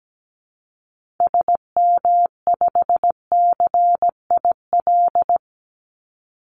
Morse code, heard as a keyed tone, about 17 wpm, 700 Hz. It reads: SM5CIL
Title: Call Signs - 17wpm